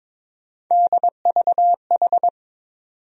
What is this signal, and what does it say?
Morse code, heard as a keyed tone, about 22 wpm, 700 Hz. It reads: DVH
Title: USA Call Sign Postfixes - 3 Letters 22wpm